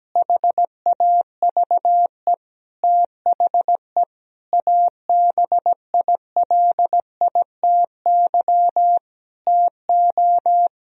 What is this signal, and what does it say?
Morse code, heard as a keyed tone, about 17 wpm, 700 Hz. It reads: HAVE THE ABILITY TO
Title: Sentences from Top 1000 Words - Review Part 3 - 17wpm